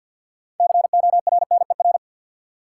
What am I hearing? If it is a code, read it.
Morse code, heard as a keyed tone, about 52 wpm, 700 Hz. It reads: CORNER
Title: Sentences from Top 1600 Words - New Word Drill - 50wpm